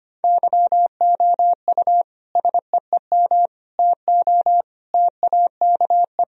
Morse code, heard as a keyed tone, about 25 wpm, 700 Hz. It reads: YOU SEEM TO TAKE
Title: Sentences from Top 1000 Words - Review Part 3 - 25wpm